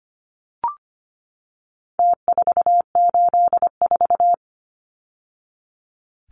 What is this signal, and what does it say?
Morse code, heard as a keyed tone, about 25 wpm, 700 Hz. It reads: T484
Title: Numbers - Contest Style - 4-Digit 25wpm